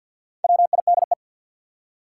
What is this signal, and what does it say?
Morse code, heard as a keyed tone, about 50 wpm, 700 Hz. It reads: WIDE
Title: Four Letter Words - 50wpm